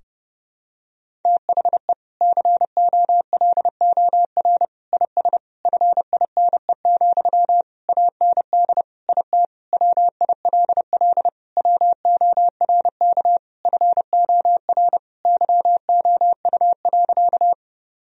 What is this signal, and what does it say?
Morse code, heard as a keyed tone, about 30 wpm, 700 Hz. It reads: THE COLOR IS FINE, AND IT WILL WORK FOR YOU.
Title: Sentences from Top 1000 Words - Review Part 2 - 30wpm